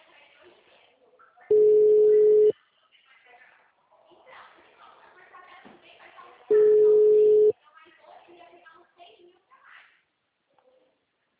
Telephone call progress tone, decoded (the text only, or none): ringback tone